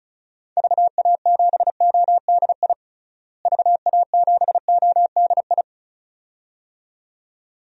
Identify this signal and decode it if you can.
Morse code, heard as a keyed tone, about 36 wpm, 700 Hz. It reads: VA7ODI VA7ODI
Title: DXCC-100 Call Signs - 6 Characters - Repeated 2x - 35wpm